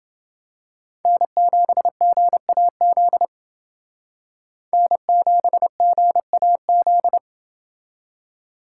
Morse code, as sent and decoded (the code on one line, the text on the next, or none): -. --... --. .- --.. / -. --... --. .- --..
N7GAZ N7GAZ